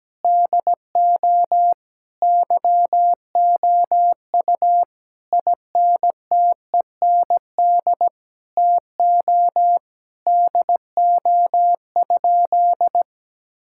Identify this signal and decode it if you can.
Morse code, heard as a keyed tone, about 17 wpm, 700 Hz. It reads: DO YOU INTEND TO DO?
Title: Sentences from Top 1600 Words 17wpm